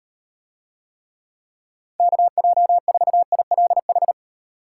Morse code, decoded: KJ4ILH

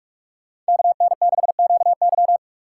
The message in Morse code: KN6QY

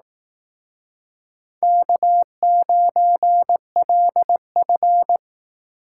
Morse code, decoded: K9LF

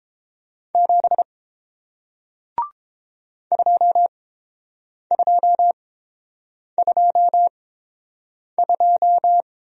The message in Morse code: 7 2222

